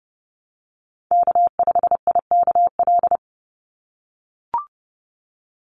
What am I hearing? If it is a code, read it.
Morse code, heard as a keyed tone, about 30 wpm, 700 Hz. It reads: K5IKL
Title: USA Call Signs 30wpm